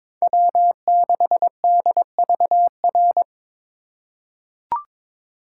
Morse code, heard as a keyed tone, about 22 wpm, 700 Hz. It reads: W6DVR
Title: USA Call Signs - 5 Characters - 2x Character Spacing - 22wpm